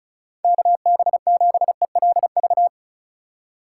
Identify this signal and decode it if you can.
Morse code, heard as a keyed tone, about 34 wpm, 700 Hz. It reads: KB7ELV
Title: USA Call Signs - 6 Characters - 4x Character Spacing - 35wpm